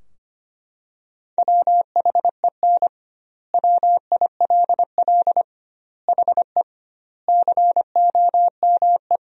Morse code, heard as a keyed tone, about 25 wpm, 700 Hz. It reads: WHEN WILL HE COME
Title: 2x Word-Spacing - Sentences from Top 200 Words - 25wpm